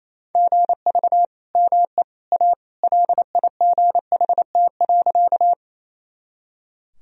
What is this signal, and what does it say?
Morse code, heard as a keyed tone, about 27 wpm, 700 Hz. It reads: GV ME A LIGHT.
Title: Sentences from Top 200 Words and Abbreviations 28wpm